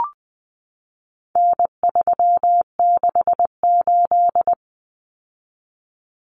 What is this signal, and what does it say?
Morse code, heard as a keyed tone, about 20 wpm, 700 Hz. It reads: N368
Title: Common Cut Numbers - 4-Digit 20wpm